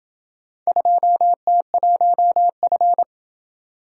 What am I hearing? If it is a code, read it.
Morse code, heard as a keyed tone, about 27 wpm, 700 Hz. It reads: 2T1F